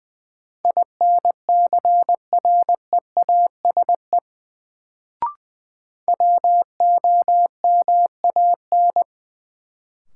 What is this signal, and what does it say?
Morse code, heard as a keyed tone, about 20 wpm, 700 Hz. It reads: INCREASE WOMAN
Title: Sets of 5 Words - New Word Drill - 20wpm